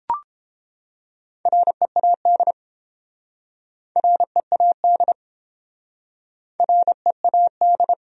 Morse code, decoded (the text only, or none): READ READ READ